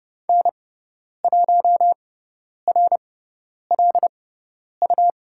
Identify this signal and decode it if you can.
Morse code, heard as a keyed tone, about 30 wpm, spaced as for 10 wpm, 700 Hz. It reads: N1RLU
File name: Lesson 17g - Letter u - Call Signs 30wpm